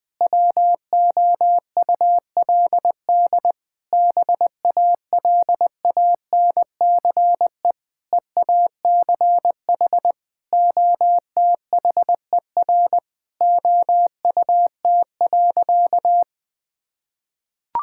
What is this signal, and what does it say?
Morse code, heard as a keyed tone, about 20 wpm, 700 Hz. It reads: WOULD BALANCE EACH OTHER OUT.
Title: Sentences from Top 1000 Words - Review Part 6 - 20wpm